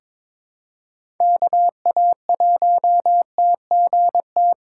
Morse code, heard as a keyed tone, about 22 wpm, 700 Hz. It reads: KA1TGT
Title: USA Call Signs - 6 Characters - 8x Character Spacing - 22wpm